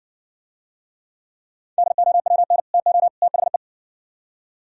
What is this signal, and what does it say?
Morse code, heard as a keyed tone, about 57 wpm, 700 Hz. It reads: DOWN TO THE